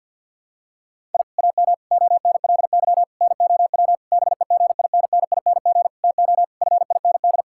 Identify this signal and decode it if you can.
Morse code, heard as a keyed tone, about 51 wpm, 700 Hz. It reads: I AM ONLY NOW BEGINNING TO FIND
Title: Sentences from Top 1000 Words - Review Part 5 - 50wpm